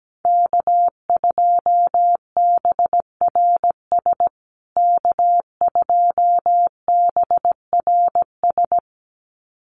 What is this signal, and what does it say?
Morse code, heard as a keyed tone, about 17 wpm, 700 Hz. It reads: K2BRS K2BRS
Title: USA Call Signs Repeated 2x 17wpm